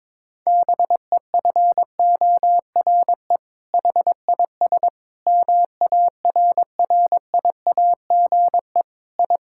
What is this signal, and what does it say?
Morse code, heard as a keyed tone, about 21 wpm, 700 Hz. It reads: BEFORE HIS MARRIAGE I
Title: Sentences from Top 1000 Words - Review Part 5 - 22wpm